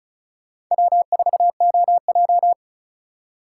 Morse code, heard as a keyed tone, about 45 wpm, 700 Hz. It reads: W4OJ